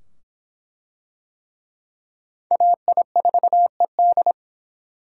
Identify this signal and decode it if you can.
Morse code, heard as a keyed tone, about 23 wpm, 700 Hz. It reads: AI4ED